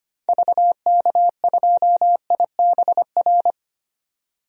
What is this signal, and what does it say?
Morse code, heard as a keyed tone, about 25 wpm, 700 Hz. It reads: VK2IBR
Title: DXCC-100 Call Signs - 6 Characters - 25wpm